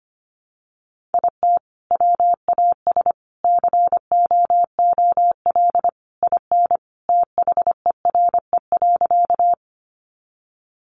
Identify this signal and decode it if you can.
Morse code, heard as a keyed tone, about 25 wpm, 700 Hz. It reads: IT WAS COOL IN THERE.
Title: Sentences from Top 1200 Words 25wpm